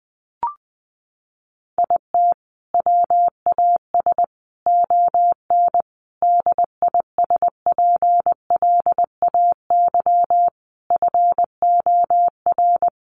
Morse code, heard as a keyed tone, about 20 wpm, 700 Hz. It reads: IT WAS ON DISPLAY FOR
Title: Sentences from Top 1200 Words 20wpm